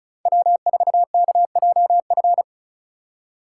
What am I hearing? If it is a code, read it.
Morse code, heard as a keyed tone, about 45 wpm, 700 Hz. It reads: W4KJF